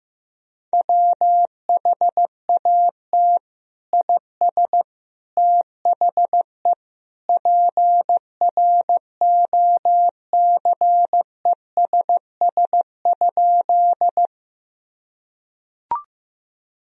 Morse code, heard as a keyed tone, about 14 wpm, 700 Hz. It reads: WHAT IS THE PROCESS?
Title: Sentences from Top 1000 Words - Review Part 3 - 15wpm